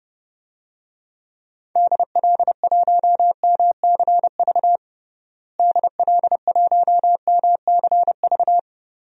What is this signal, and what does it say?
Morse code, heard as a keyed tone, about 30 wpm, 700 Hz. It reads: DL1MCV DL1MCV
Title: DXCC-100 Call Signs - 6 Characters - Repeated 3x - 30wpm